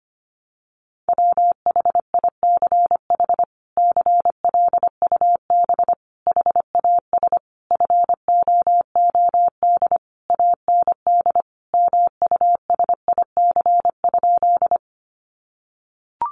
Morse code, heard as a keyed tone, about 25 wpm, 700 Hz. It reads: WHICH CLUB HAS FOOD AND MUSIC?
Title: Sentences from Top 1200 Words 25wpm